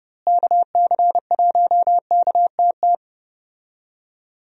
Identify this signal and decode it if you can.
Morse code, heard as a keyed tone, about 31 wpm, 700 Hz. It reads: KC1KTT
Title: USA Call Signs - 6 Characters - Repeated 3x - 30wpm